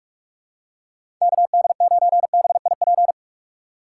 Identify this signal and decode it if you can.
Morse code, heard as a keyed tone, about 46 wpm, 700 Hz. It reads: KD9BIP
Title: USA Call Signs - 6 Characters - 2x Character Spacing - 45wpm